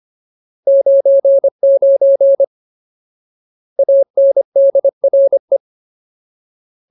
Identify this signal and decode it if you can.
Morse code, heard as a keyed tone, about 25 wpm, 550 Hz. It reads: 99 ANDRE
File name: Ultimate Word List / 25WPM Ultimate